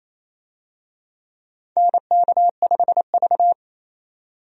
Morse code, heard as a keyed tone, about 28 wpm, 700 Hz. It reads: NK5V
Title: USA Call Signs - 4 Characters - 28wpm